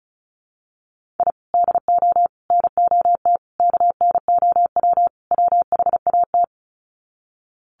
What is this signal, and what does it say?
Morse code, heard as a keyed tone, about 34 wpm, 700 Hz. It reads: I DO NOT KNOW WHAT
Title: Sets of 5 Words 35wpm